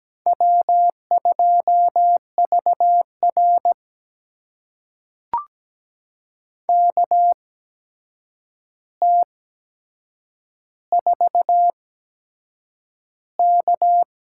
Morse code, heard as a keyed tone, about 17 wpm, spaced as for 7 wpm, 700 Hz. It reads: W2VR KT4K
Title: USA Call Signs - 4 Characters - 8x Character Spacing - 17wpm